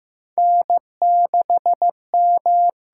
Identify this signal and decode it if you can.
Morse code, heard as a keyed tone, about 15 wpm, 700 Hz. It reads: N6M
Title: USA Call Signs - 6x Character Spacing - 15wpm